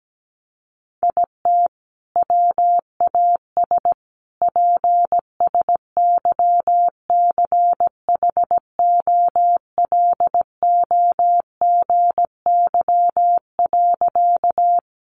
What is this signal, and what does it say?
Morse code, heard as a keyed tone, about 17 wpm, 700 Hz. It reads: IT WAS PSYCHOLOGY.
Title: Sentences from Top 1400 Words 17wpm